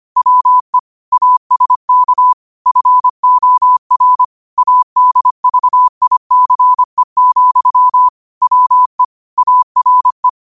Morse code, heard as a keyed tone, about 25 wpm, 1000 Hz. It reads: WE ASK FOR ADVICE, WE ARE
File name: qotdfiles / 25wpm